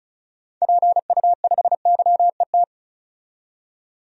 Morse code, heard as a keyed tone, about 34 wpm, 700 Hz. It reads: PU5YET
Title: DXCC-100 Call Signs - 6 Characters - 35wpm